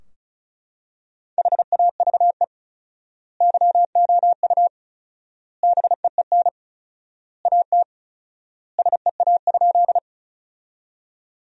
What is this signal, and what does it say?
Morse code, heard as a keyed tone, about 34 wpm, 700 Hz. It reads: HAVE YOU BEEN AT SEA?
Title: 4x Word-Spacing - Sentences from Top 300 Words - 35wpm